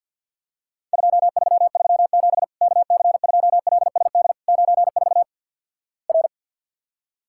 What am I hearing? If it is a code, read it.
Morse code, heard as a keyed tone, about 52 wpm, 700 Hz. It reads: 2337 KC2LSD 94 R
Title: Worked All Europe DX Contest - QTC Exchange with Repeats - 50wpm